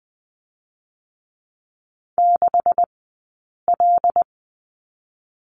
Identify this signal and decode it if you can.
Morse code, heard as a keyed tone, about 20 wpm, spaced as for 8 wpm, 700 Hz. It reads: 6L
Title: 126 Lesson 21d - Number 6 - Two Characters 20wpm